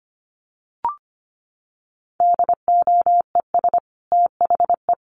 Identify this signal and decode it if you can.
Morse code, heard as a keyed tone, about 25 wpm, 700 Hz. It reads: DOES THE
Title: Sentences from Top 1500 Words 25wpm